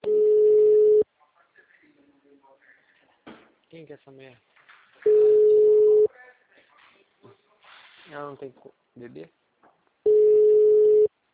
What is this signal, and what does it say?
Ringback tone (Europe, 425 Hz), with speech.
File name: out- -2009- - - .5